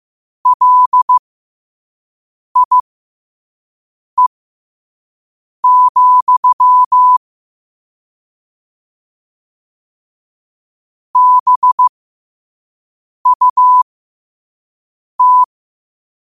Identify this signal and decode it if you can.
Morse code, heard as a keyed tone, about 15 wpm, spaced as for 5 wpm, 1000 Hz. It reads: LIE, BUT
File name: qotdfiles / 5wpm